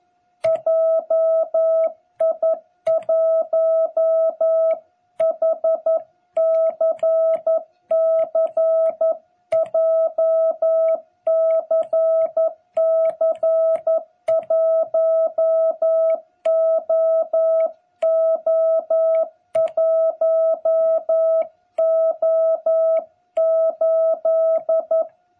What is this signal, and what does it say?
Morse code, heard as a keyed tone, about 11 wpm, 660 Hz. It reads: JI1HCC JCC1OO1O8
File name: jh8cbh / 2021cw